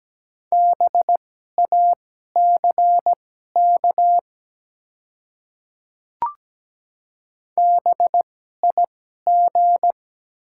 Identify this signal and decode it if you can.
Morse code, heard as a keyed tone, about 17 wpm, spaced as for 12 wpm, 700 Hz. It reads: BACK BIG
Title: 2x Character Spacing - Top 300 Words - 17wpm